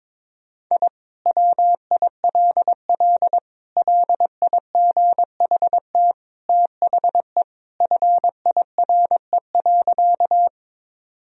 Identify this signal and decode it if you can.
Morse code, heard as a keyed tone, about 22 wpm, 700 Hz. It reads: I WILL LIGHT THE FIRE.
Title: Sentences from Top 1000 Words - Review Part 1 - 22wpm